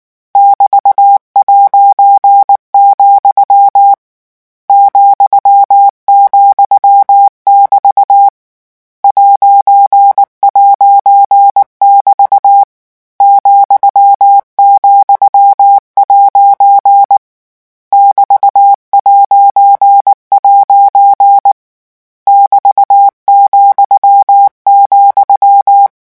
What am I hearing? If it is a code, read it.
Morse code, heard as a keyed tone, about 19 wpm, 800 Hz. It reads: <BT>', ,,<BT> ''<BT> ,,' <BT>'' <BT>,,
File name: morse mp3 / lecon14-vitesse 18